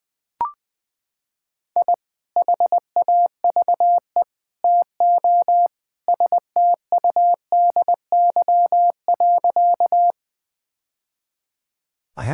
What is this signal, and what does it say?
Morse code, heard as a keyed tone, about 20 wpm, 700 Hz. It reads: I HAVE TO STUDY.
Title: Sentences with 3-Word Phrases - 20wpm